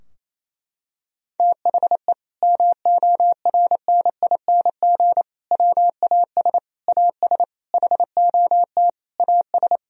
Morse code, heard as a keyed tone, about 28 wpm, 700 Hz. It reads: THE MORNING WAS AS HOT AS
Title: Sentences from Top 1000 Words - Review Part 3 - 28wpm